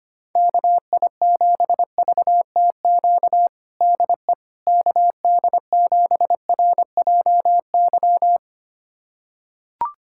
Morse code, heard as a keyed tone, about 25 wpm, 700 Hz. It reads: KI7VTQ DE KD7RJY
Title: USA Callsign DE Callsign - 6 Characters - 25wpm